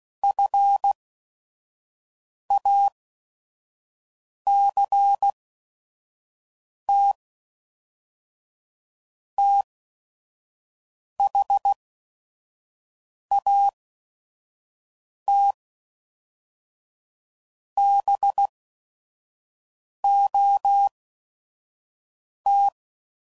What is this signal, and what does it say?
Morse code, heard as a keyed tone, about 16 wpm, spaced as for 5 wpm, 800 Hz. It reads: FACT THAT BOT
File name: CW Code Groups @5WPM / Technical Subjects / Reactance